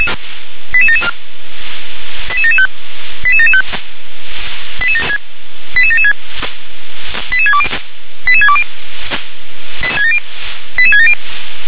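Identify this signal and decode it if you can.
Five-tone selective call (ZVEI), heard as quick runs of five tones, tones of about 70 ms. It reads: 88774 88775 88775 88776 88620 88620 88670
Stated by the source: Five-tone ZVEI selective call received by SDR.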